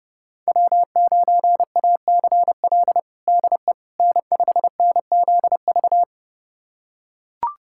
Morse code, heard as a keyed tone, about 30 wpm, 700 Hz. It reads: W9ACL DE N5NZV
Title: USA Callsign DE Callsign - 5 Characters - 30wpm